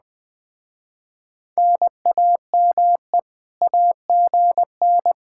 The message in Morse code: NAME AGN

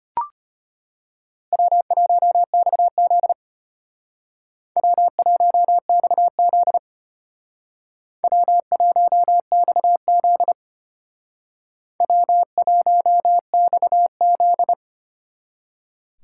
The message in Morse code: W1XZ W1XZ W1XZ W1XZ

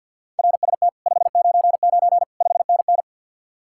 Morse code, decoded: RST 599 5NN

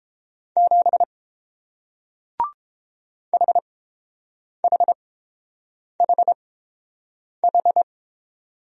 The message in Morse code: 7 HHHH

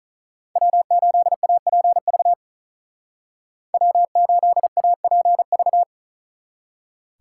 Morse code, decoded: W8APV W8APV

